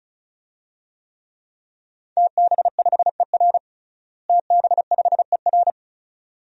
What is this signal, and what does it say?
Morse code, heard as a keyed tone, about 36 wpm, 700 Hz. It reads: TB5ER TB5ER
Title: DXCC-100 Call Signs - 5 Characters - Repeated 2x - 35wpm